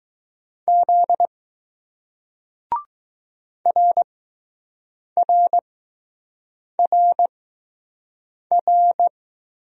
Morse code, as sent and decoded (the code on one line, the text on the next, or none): --.. / .-. .-. .-. .-.
Z RRRR